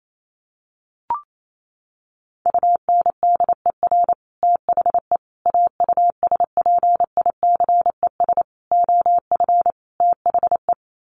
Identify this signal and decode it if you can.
Morse code, heard as a keyed tone, about 28 wpm, 700 Hz. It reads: UNDER THE AUSPICES OF THE
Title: Sets of 5 Words Encore2 28wpm